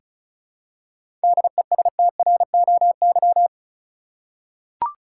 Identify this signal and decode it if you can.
Morse code, heard as a keyed tone, about 34 wpm, 700 Hz. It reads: DESTROY
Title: Seven Letter Words - 35wpm